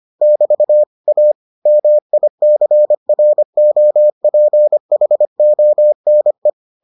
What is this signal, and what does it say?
Morse code, heard as a keyed tone, about 25 wpm, 600 Hz. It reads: <BT> A MICROPHONE